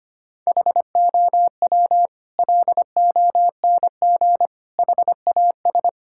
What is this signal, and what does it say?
Morse code, heard as a keyed tone, about 25 wpm, 700 Hz. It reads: HOW LONG HAS
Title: Sentences from Top 1000 Words - Review Part 6 - 25wpm